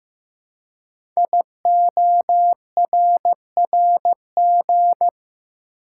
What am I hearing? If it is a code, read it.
Morse code, heard as a keyed tone, about 15 wpm, 700 Hz. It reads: IORRG
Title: Code Groups - 5 Letters - 15wpm